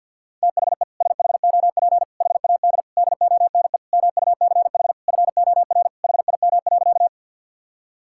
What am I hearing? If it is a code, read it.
Morse code, heard as a keyed tone, about 50 wpm, 700 Hz. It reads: THE SHOP HAD DONE MUCH FOR HIM.
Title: Sentences from Top 1600 Words 50wpm